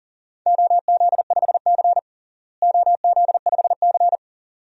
Morse code, heard as a keyed tone, about 37 wpm, 700 Hz. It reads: OZ5C OZ5C
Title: DXCC-100 Call Signs - 4 Characters - Repeated 2x - 40wpm